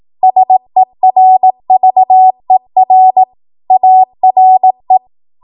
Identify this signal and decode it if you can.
Morse code, heard as a keyed tone, about 18 wpm, 750 Hz. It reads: SERVER ARE
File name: Archive / 18 WPM / 18WPM